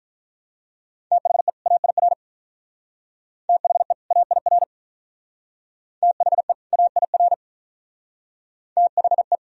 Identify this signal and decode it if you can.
Morse code, heard as a keyed tone, about 45 wpm, 700 Hz. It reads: THE AIR THE AIR THE AIR THE